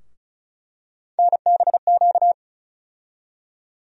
Morse code, heard as a keyed tone, about 34 wpm, 700 Hz. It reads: NBQ
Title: USA Call Sign Postfixes - 3 Letters 35wpm